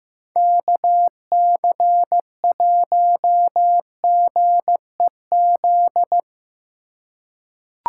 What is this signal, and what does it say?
Morse code, heard as a keyed tone, about 16 wpm, 700 Hz. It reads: KC1GEZ
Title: USA Call Signs - 3x Character Spacing - 15wpm